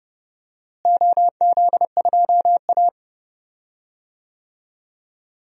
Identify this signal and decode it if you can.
Morse code, heard as a keyed tone, about 31 wpm, 700 Hz. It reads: OZ2A
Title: DXCC-100 Call Signs - 4 Characters - Repeated 3x - 30wpm